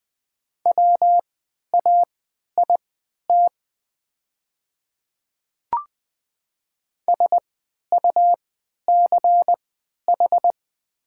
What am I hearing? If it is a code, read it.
Morse code, heard as a keyed tone, about 20 wpm, spaced as for 11 wpm, 700 Hz. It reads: WAIT SUCH
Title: 3x Character Spacing - Top 500 Words - 20wpm